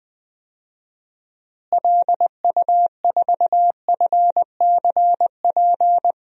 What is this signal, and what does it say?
Morse code, heard as a keyed tone, about 20 wpm, 700 Hz. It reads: LU4FCP
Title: DXCC-100 Call Signs - 6 Characters - Repeated 2x - 20wpm